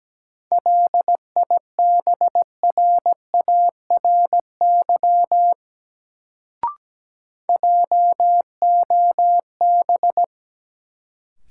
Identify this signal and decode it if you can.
Morse code, heard as a keyed tone, about 17 wpm, 700 Hz. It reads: LIBRARY JOB
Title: Sentences from Top 600 Words - New Word Drill - 17wpm